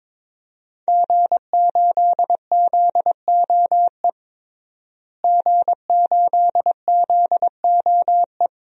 Morse code, heard as a keyed tone, about 22 wpm, 700 Hz. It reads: G8ZOE G8ZOE
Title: DXCC-100 Call Signs - 5 Characters - Repeated 2x - 22wpm